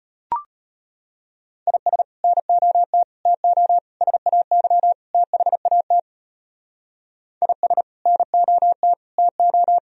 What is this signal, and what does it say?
Morse code, heard as a keyed tone, about 32 wpm, 700 Hz. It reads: IS NOT TO SAY THAT IS NOT TO